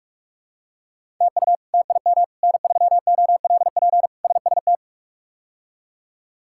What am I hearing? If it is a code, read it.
Morse code, heard as a keyed tone, about 46 wpm, 700 Hz. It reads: TU TIM N3OLP SST
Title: K1USN SST QSO Practice - 45wpm